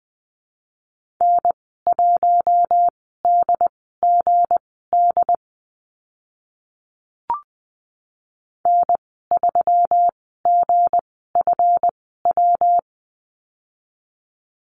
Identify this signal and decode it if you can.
Morse code, heard as a keyed tone, about 19 wpm, spaced as for 14 wpm, 700 Hz. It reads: N1DGD N3GFW
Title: 161 Lesson 26g - Letter g - Call Signs 20wpm